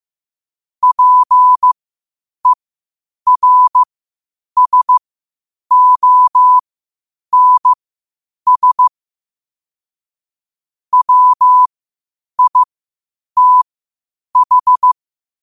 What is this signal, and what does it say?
Morse code, heard as a keyed tone, about 15 wpm, spaced as for 8 wpm, 1000 Hz. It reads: PERSONS WITH
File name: qotdfiles / 8wpm